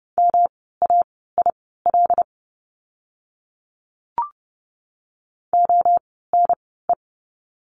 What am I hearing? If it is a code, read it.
Morse code, heard as a keyed tone, about 30 wpm, spaced as for 17 wpm, 700 Hz. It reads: MAIL ONE
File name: course / 30:10wpm / 108 Lesson 18f - Letter m - Words 30wpm